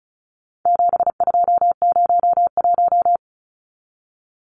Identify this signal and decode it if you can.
Morse code, heard as a keyed tone, about 34 wpm, 700 Hz. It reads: 7201
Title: Numbers - 4-Digit 35wpm